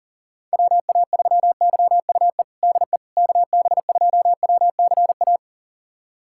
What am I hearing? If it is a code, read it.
Morse code, heard as a keyed tone, about 40 wpm, 700 Hz. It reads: WA3YUE DE KB2WCA
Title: USA Callsign DE Callsign 40wpm